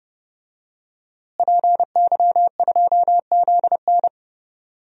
Morse code, heard as a keyed tone, about 30 wpm, 700 Hz. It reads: PY2ZN
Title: DXCC-100 Call Signs - 5 Characters - Repeated 3x - 30wpm